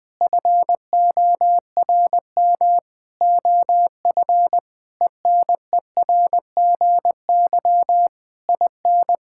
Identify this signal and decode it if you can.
Morse code, heard as a keyed tone, about 19 wpm, 700 Hz. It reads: FORM OF ENERGY IN
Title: Sentences from Top 1000 Words - Review Part 6 - 20wpm